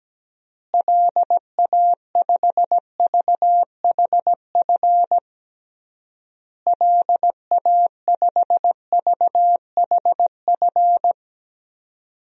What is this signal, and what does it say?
Morse code, heard as a keyed tone, about 17 wpm, 700 Hz. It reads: LA5VHF LA5VHF
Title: DXCC-100 Call Signs - 6 Characters - Repeated 3x - 17wpm